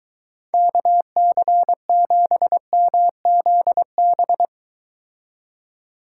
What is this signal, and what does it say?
Morse code, heard as a keyed tone, about 23 wpm, 700 Hz. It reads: KC7MZB